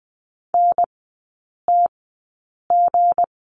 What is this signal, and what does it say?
Morse code, heard as a keyed tone, about 20 wpm, spaced as for 8 wpm, 700 Hz. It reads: NTG
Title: 166 Lesson 27e - Number 7 - Three Characters 20wpm